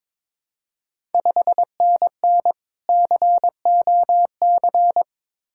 Morse code, heard as a keyed tone, about 22 wpm, 690 Hz. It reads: 5NN COC
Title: Ontario QSO Party Exchange 22wpm